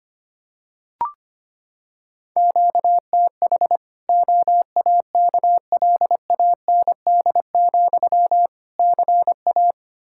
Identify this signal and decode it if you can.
Morse code, heard as a keyed tone, about 25 wpm, 700 Hz. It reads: QTH OAKLAND, CA
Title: K1USN SST QSO Practice 25wpm